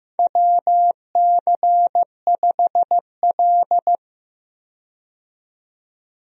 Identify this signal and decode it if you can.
Morse code, heard as a keyed tone, about 15 wpm, 700 Hz. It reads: WC5L
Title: USA Call Signs - 4 Characters - 15wpm